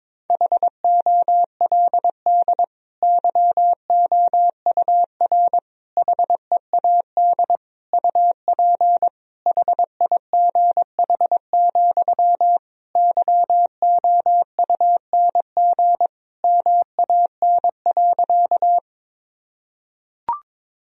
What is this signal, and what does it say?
Morse code, heard as a keyed tone, about 22 wpm, 700 Hz. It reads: HOLD YOUR HEAD UP HIGH, YOUNG MAN.
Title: Sentences from Top 1000 Words - Review Part 2 - 22wpm